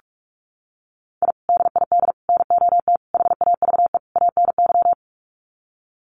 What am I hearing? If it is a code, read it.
Morse code, heard as a keyed tone, about 44 wpm, 700 Hz. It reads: I DID NOT HAVE ANY
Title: Sets of 5 Words Encore 45wpm